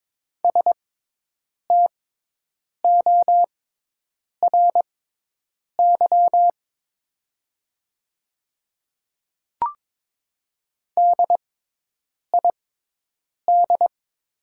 Morse code, heard as a keyed tone, about 22 wpm, spaced as for 8 wpm, 700 Hz. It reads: STORY DID
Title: 6x Character Spacing - Top 300 Words - 22wpm